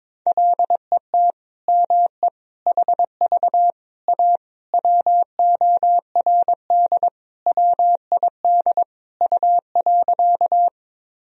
Morse code, heard as a keyed tone, about 21 wpm, 700 Hz. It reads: LET ME HV A WORD WID U.
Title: Sentences from Top 300 Words and Abbreviations 22wpm